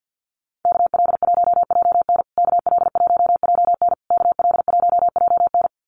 Morse code, heard as a keyed tone, about 50 wpm, 700 Hz. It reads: KL1JN KL1JN KL1JN
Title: USA Call Signs Repeated 3x 50wpm